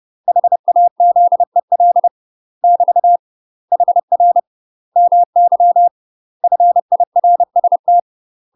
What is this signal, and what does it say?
Morse code, heard as a keyed tone, about 30 wpm, 700 Hz. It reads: HAZEL <BT> HR MY FIRST
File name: UK MASTER QSOs TRAINING / 30WPM / UK MASTER QSOs 0025